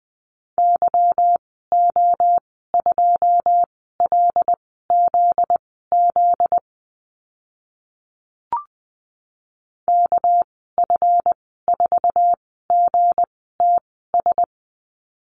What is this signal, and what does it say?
Morse code, heard as a keyed tone, about 20 wpm, spaced as for 14 wpm, 700 Hz. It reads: YO2LZZ KF4GTS
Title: 245 Lesson 39g - Letter z - Call Signs 20wpm